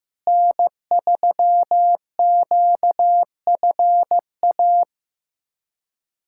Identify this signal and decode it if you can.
Morse code, heard as a keyed tone, about 15 wpm, 700 Hz. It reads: N3QFA
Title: USA Callsign DE Callsign - 5 Characters - 15wpm